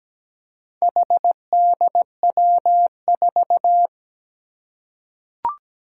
Morse code, heard as a keyed tone, about 17 wpm, 700 Hz. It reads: HDW4
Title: Four Challenging Letter-Number Combinations 17wpm